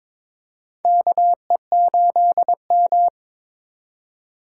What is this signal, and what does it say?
Morse code, heard as a keyed tone, about 23 wpm, 700 Hz. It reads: KE8M
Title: K1USN SST QSO Practice - 22wpm